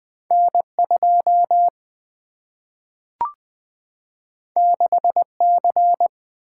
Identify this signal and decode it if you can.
Morse code, heard as a keyed tone, about 20 wpm, 700 Hz. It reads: N 2 6 C
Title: Single-Letter Single-Number 20wpm